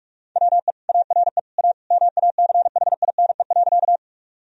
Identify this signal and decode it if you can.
Morse code, heard as a keyed tone, about 46 wpm, 700 Hz. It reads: WE ARE A MACHINE.
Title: Sentences from Top 1000 Words - Review Part 2 - 45wpm